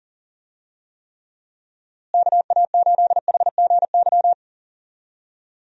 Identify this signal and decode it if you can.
Morse code, heard as a keyed tone, about 40 wpm, 700 Hz. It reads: KA8HGY
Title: USA Call Signs - 6 Characters - Repeated 2x - 40wpm